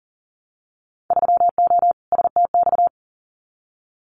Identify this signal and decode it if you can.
Morse code, heard as a keyed tone, about 40 wpm, 700 Hz. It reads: 3O STX
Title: Winter Field Day Exchanges 40wpm